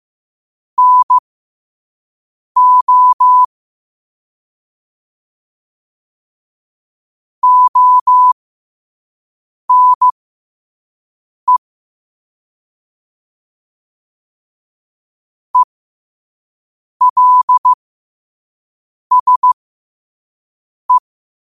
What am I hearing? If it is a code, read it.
Morse code, heard as a keyed tone, about 15 wpm, spaced as for 5 wpm, 1000 Hz. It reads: NO ONE ELSE
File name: qotdfiles / 5wpm